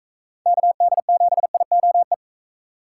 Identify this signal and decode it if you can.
Morse code, heard as a keyed tone, about 41 wpm, 700 Hz. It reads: KD7IOE